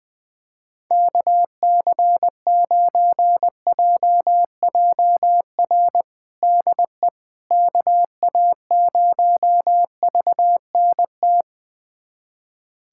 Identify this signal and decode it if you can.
Morse code, heard as a keyed tone, about 20 wpm, 700 Hz. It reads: KC9JJR DE KA0VNT
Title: USA Callsign DE Callsign - 6 Characters - 20wpm